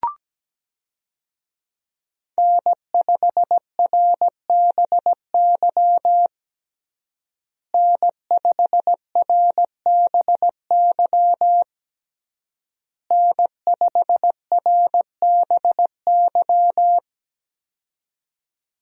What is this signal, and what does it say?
Morse code, heard as a keyed tone, about 17 wpm, 700 Hz. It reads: N5RBY N5RBY N5RBY
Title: USA Call Signs - 5 Characters - Repeated 3x - 17wpm